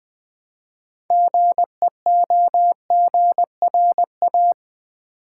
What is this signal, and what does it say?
Morse code, heard as a keyed tone, about 20 wpm, 700 Hz. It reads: GEOGRA
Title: Build A Word - 3 Letters At A Time - 9-Letter Words - 20wpm